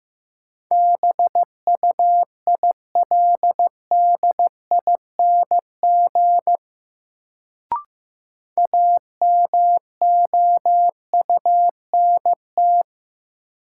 Morse code, heard as a keyed tone, about 15 wpm, 700 Hz. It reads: BUILDING AMOUNT
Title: Sentences from Top 700 Words - New Word Drill - 15wpm